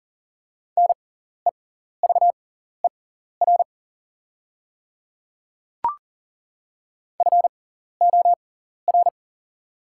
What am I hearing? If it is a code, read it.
Morse code, heard as a keyed tone, about 40 wpm, spaced as for 14 wpm, 690 Hz. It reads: NEVER FOR
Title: 6x Character Spacing - Top 300 Words - 40wpm